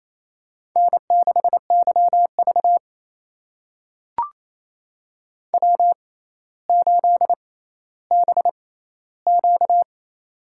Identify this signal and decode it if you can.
Morse code, heard as a keyed tone, about 28 wpm, spaced as for 14 wpm, 700 Hz. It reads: N6YV W8BQ
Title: USA Call Signs - 4 Characters - 6x Character Spacing - 28wpm